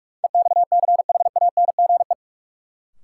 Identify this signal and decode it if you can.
Morse code, heard as a keyed tone, about 44 wpm, 700 Hz. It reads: EXCHANGE
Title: Sentences from Top 1000 Words - Review Part 6 - New Word Drill - 45wpm